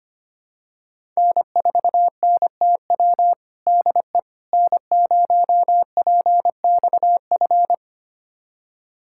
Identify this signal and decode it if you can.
Morse code, heard as a keyed tone, about 25 wpm, 700 Hz. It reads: N4NTW DE N0PXF
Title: USA Callsign DE Callsign - 5 Characters - 25wpm